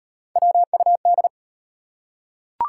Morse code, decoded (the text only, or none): WUD